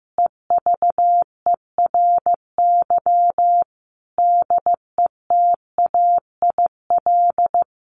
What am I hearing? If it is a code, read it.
Morse code, heard as a keyed tone, about 15 wpm, 700 Hz. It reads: EVERY DETAIL
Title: Sentences from Top 1500 Words 15wpm